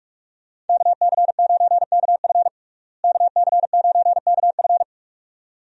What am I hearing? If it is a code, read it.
Morse code, heard as a keyed tone, about 42 wpm, 700 Hz. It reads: KC9KF KC9KF
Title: USA Call Signs - 5 Characters - Repeated 2x - 45wpm